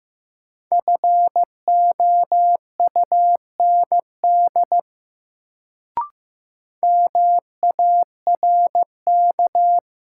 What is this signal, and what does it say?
Morse code, heard as a keyed tone, about 15 wpm, 700 Hz. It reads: FOUND MARK
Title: Sentences from Top 300 Words - New Word Drill - 15wpm